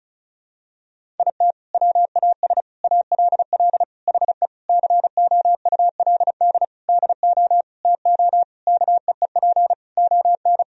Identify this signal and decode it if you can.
Morse code, heard as a keyed tone, about 34 wpm, 700 Hz. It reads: IT WAS ALL HE COULD DO TO KEEP ON
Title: Sentences from Top 1000 Words - Review Part 1 - 35wpm